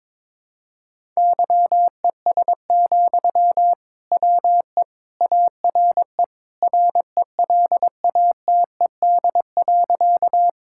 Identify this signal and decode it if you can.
Morse code, heard as a keyed tone, about 22 wpm, 700 Hz. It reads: YES, WE ARE RELATED.
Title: Sentences from Top 1000 Words - Review Part 5 - 22wpm